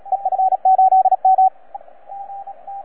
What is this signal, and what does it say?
Morse code, heard as a keyed tone, about 33 wpm, 750 Hz, with radio noise. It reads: EF8M
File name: EF8M-WAEDC-CW-2010-20-CW